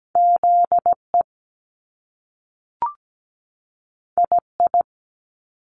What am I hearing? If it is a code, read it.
Morse code, heard as a keyed tone, about 17 wpm, 700 Hz. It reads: ZE II
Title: Top Two Letter Combinations in Words 17wpm